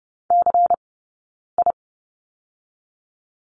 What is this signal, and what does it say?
Morse code, heard as a keyed tone, about 30 wpm, 700 Hz. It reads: C I
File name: Lesson 18d - Letter m - Two Characters 30wpm